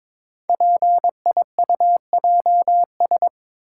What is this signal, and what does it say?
Morse code, heard as a keyed tone, about 22 wpm, 700 Hz. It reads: PIUJS
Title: Code Groups - 5 Letters - 22wpm